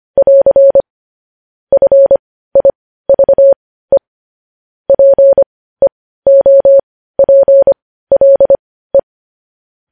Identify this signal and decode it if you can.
Morse code, heard as a keyed tone, about 25 wpm, spaced as for 15 wpm, 550 Hz. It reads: <AR> FIVE PEOPLE
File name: podcast / News headlines-1x-25x15